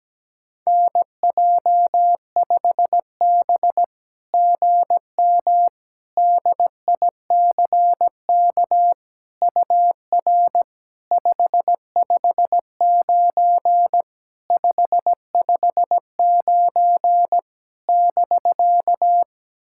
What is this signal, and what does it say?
Morse code, heard as a keyed tone, about 17 wpm, 700 Hz. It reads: NJ5B GM DICK UR 559 559 <BK>
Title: SOTA QSO Practice 17wpm